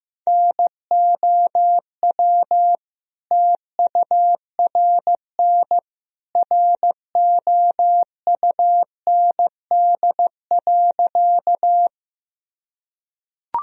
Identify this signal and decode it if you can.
Morse code, heard as a keyed tone, about 15 wpm, 700 Hz. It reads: NOW TURN ROUND.
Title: Sentences from Top 1000 Words - Review Part 1 - 15wpm